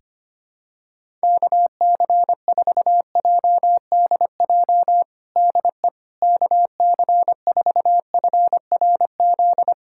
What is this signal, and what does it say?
Morse code, heard as a keyed tone, about 25 wpm, 700 Hz. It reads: KC4JDJ DE KC4FRZ
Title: USA Callsign DE Callsign - 6 Characters - 25wpm